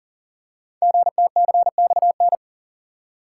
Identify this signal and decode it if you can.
Morse code, heard as a keyed tone, about 39 wpm, 700 Hz. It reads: GTCXN
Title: Code Groups - 5 Letters - 40wpm